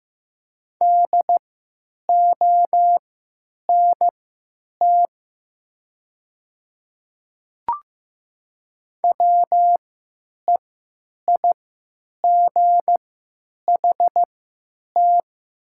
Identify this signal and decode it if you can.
Morse code, heard as a keyed tone, about 15 wpm, spaced as for 9 wpm, 700 Hz. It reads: DONT WEIGHT
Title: 3x Character Spacing - Top 500 Words - 15wpm